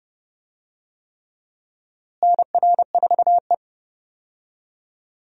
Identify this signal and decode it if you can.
Morse code, heard as a keyed tone, about 32 wpm, 700 Hz. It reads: NR4E
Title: USA Call Signs - 4 Characters - Repeated 3x - 30wpm